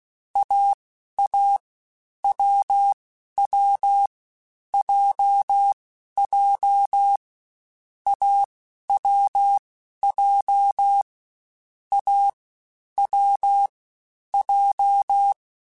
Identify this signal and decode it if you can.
Morse code, heard as a keyed tone, about 16 wpm, spaced as for 11 wpm, 800 Hz. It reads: AA WW JJ AWJ A W J
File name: Short Runs / Short Runs-mp3s / AWJ @15